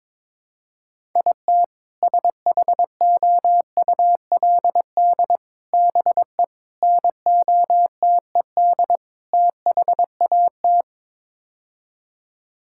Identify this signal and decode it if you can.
Morse code, heard as a keyed tone, about 22 wpm, 700 Hz. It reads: IT SHOULD BE NOTED THAT
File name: practice / 22wpm / Build A Phrase - 5-Word Phrases - 22wpm